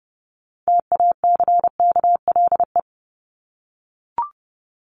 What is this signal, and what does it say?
Morse code, heard as a keyed tone, about 30 wpm, 700 Hz. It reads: TACKLE
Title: Top 1751-2000 Words 30wpm